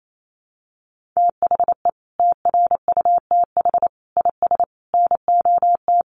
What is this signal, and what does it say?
Morse code, heard as a keyed tone, about 28 wpm, 700 Hz. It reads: THE TRUTH IS NOT
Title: Sentences from Top 1500 Words 28wpm